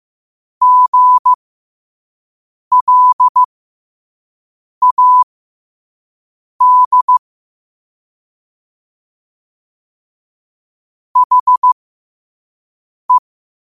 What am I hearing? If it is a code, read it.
Morse code, heard as a keyed tone, about 15 wpm, spaced as for 5 wpm, 1000 Hz. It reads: GLAD HE
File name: 5wpm